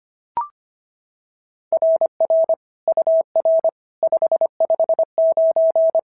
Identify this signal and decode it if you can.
Morse code, heard as a keyed tone, about 25 wpm, 690 Hz. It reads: RR UR 559
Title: SOTA QSO Practice 25wpm